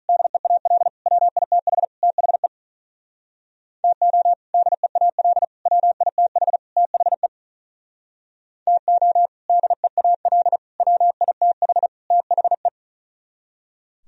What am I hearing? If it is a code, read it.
Morse code, heard as a keyed tone, about 46 wpm, 700 Hz. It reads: DEAL WITH THE TO DEAL WITH THE TO DEAL WITH THE